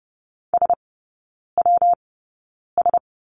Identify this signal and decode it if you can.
Morse code, heard as a keyed tone, about 30 wpm, spaced as for 9 wpm, 700 Hz. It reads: SWS
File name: Lesson 19e - Letter w - Three Characters 30wpm